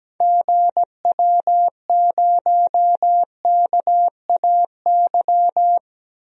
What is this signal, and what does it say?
Morse code, heard as a keyed tone, about 17 wpm, 700 Hz. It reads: GW0KAY
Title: DXCC-100 Call Signs - 6 Characters - 17wpm